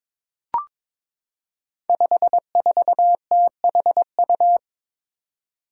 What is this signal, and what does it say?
Morse code, heard as a keyed tone, about 22 wpm, 700 Hz. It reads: 54THU
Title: Five Challenging Letter-Number Combinations 22wpm